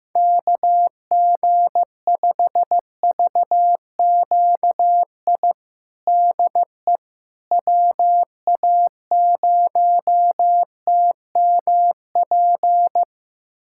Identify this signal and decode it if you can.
Morse code, heard as a keyed tone, about 15 wpm, 700 Hz. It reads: KG5VQI DE WA0TMP
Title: USA Callsign DE Callsign 15wpm